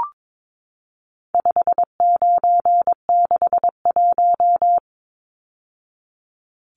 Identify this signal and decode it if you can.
Morse code, heard as a keyed tone, about 22 wpm, 700 Hz. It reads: 5961
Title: Numbers - 4-Digit 22wpm